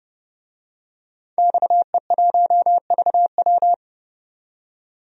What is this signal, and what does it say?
Morse code, heard as a keyed tone, about 30 wpm, 690 Hz. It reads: XE1VW
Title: DXCC-100 Call Signs - 5 Characters - 30wpm